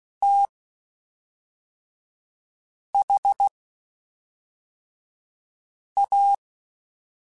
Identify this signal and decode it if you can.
Morse code, heard as a keyed tone, about 16 wpm, spaced as for 3 wpm, 800 Hz. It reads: THA
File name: (spaced) / QSOs Spaced-mp3s / WA7QOX 1 @5 sp